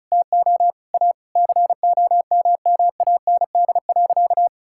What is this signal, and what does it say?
Morse code, heard as a keyed tone, about 36 wpm, 700 Hz. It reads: TO A COMMAND.
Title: Sentences from Top 1600 Words 35wpm